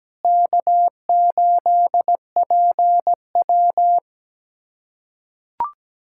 Morse code, heard as a keyed tone, about 17 wpm, 700 Hz. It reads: K8PW
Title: USA Callsign DE Callsign - 4 Characters - 17wpm